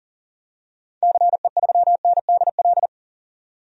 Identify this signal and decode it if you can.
Morse code, heard as a keyed tone, about 40 wpm, 700 Hz. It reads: CE3NDL
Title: DXCC-100 Call Signs - 6 Characters - 40wpm